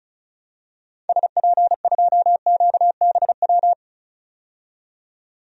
Morse code, heard as a keyed tone, about 36 wpm, 700 Hz. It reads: SP2QBW
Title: DXCC-100 Call Signs - 6 Characters - 35wpm